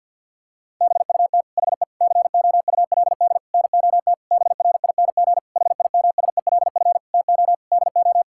Morse code, heard as a keyed tone, about 50 wpm, 700 Hz. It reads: BUT HE COULD NOT BRING HIMSELF TO DO
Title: Sentences from Top 1600 Words 50wpm